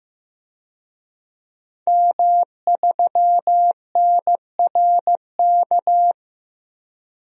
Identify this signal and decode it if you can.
Morse code, heard as a keyed tone, about 16 wpm, 700 Hz. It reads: M3NRK
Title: DXCC-100 Call Signs - 5 Characters - Repeated 3x - 15wpm